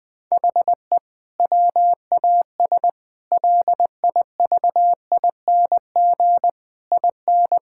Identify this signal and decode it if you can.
Morse code, heard as a keyed tone, about 20 wpm, 700 Hz. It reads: HE WAS LIVING IN
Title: Sentences with 6-Word Phrases - 20wpm